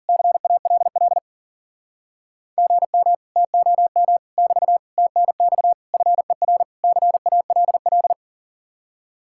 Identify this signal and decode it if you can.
Morse code, heard as a keyed tone, about 52 wpm, 700 Hz. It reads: CALL GM TOM <BT> TNX FER CALL